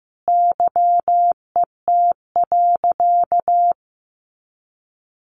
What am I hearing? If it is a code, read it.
Morse code, heard as a keyed tone, about 15 wpm, 700 Hz. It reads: YET.
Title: Sentences from Top 1100 Words 15wpm